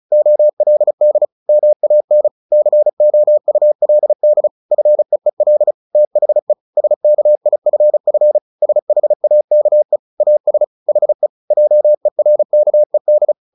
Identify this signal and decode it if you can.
Morse code, heard as a keyed tone, about 34 wpm, 600 Hz. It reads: OLD MAN COULD FEEL THE SKIFF SHAKE AS HE JERKED
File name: the old man and the sea 2 35wpm